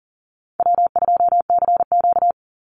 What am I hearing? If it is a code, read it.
Morse code, heard as a keyed tone, about 40 wpm, 700 Hz. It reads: W2CQ
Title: Call Signs Repeated 3x - 40wpm